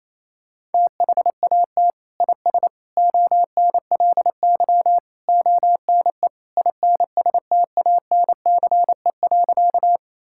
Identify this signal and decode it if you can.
Morse code, heard as a keyed tone, about 28 wpm, 700 Hz. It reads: THAT IS ONLY ONE INSTANCE.
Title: Sentences from Top 1000 Words - Review Part 5 - 28wpm